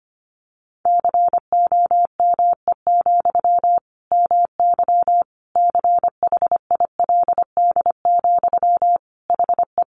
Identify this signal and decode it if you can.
Morse code, heard as a keyed tone, about 25 wpm, 700 Hz. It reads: COME, MY CHILD, HE
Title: Sentences from Top 800 Words 25wpm